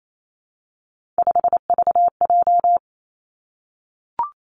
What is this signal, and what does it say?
Morse code, heard as a keyed tone, about 28 wpm, 700 Hz. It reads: 5VJ
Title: Three Challenging Letter-Number Combinations 28wpm